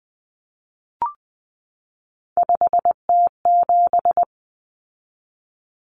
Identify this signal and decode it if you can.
Morse code, heard as a keyed tone, about 19 wpm, 700 Hz. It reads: 5T7
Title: Common Cut Numbers - 3-Digit 20wpm